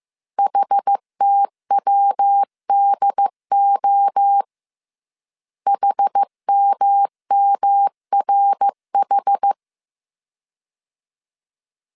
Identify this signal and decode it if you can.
Morse code, heard as a keyed tone, about 15 wpm, 800 Hz. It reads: HTWDO HMMRH